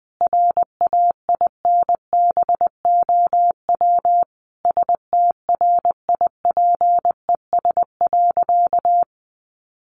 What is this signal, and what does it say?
Morse code, heard as a keyed tone, about 20 wpm, 700 Hz. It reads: RAINBOW STRIPES.
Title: Additional Easy Phrases 20wpm